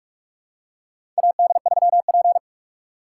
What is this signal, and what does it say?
Morse code, heard as a keyed tone, about 42 wpm, 700 Hz. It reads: AD3P
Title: USA Call Signs - 4 Characters - 6x Character Spacing - 45wpm